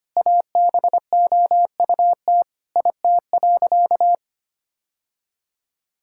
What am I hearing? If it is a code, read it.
Morse code, heard as a keyed tone, about 25 wpm, 700 Hz. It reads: ABOUT IT.
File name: practice / 25wpm / Sentences with 2-Word Phrases - 25wpm